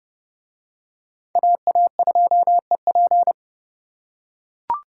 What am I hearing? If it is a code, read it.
Morse code, heard as a keyed tone, about 29 wpm, 700 Hz. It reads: AA2EP